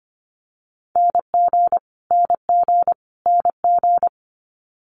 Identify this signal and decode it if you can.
Morse code, heard as a keyed tone, about 25 wpm, 700 Hz. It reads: NG NG NG
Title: Multi-letter Phonemes Repeated 3x 25wpm